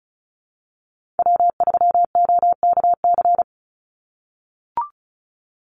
Morse code, heard as a keyed tone, about 34 wpm, 700 Hz. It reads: W3OKC
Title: Call Signs - 35wpm